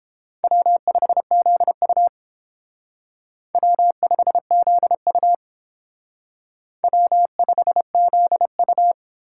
Morse code, heard as a keyed tone, about 31 wpm, 700 Hz. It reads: W5ZU W5ZU W5ZU